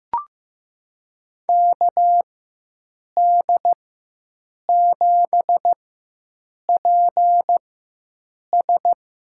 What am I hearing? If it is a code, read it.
Morse code, heard as a keyed tone, about 15 wpm, spaced as for 7 wpm, 700 Hz. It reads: KD7PS
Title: USA Call Signs - 4x Character Spacing - 15wpm